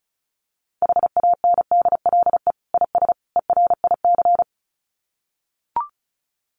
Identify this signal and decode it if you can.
Morse code, heard as a keyed tone, about 34 wpm, 700 Hz. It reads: HANDLE IS ERIC
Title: Common QSO Phrases 35wpm